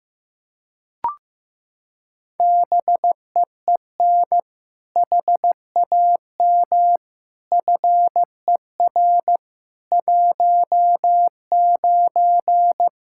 Morse code, heard as a keyed tone, about 15 wpm, 700 Hz. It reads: BEEN HAM FER 19
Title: Practice QSO - Exchange 3 - More Info - 15wpm